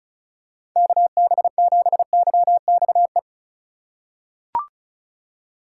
Morse code, heard as a keyed tone, about 36 wpm, 700 Hz. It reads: KB7YXE
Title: USA Call Signs - 6 Characters - 3x Character Spacing - 35wpm